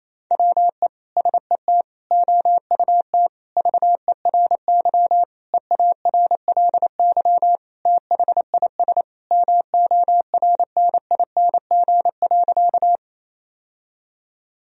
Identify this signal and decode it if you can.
Morse code, heard as a keyed tone, about 27 wpm, 700 Hz. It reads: WE SET OUT VERY EARLY THIS MORNING.
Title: Sentences from Top 1000 Words - Review Part 2 - 28wpm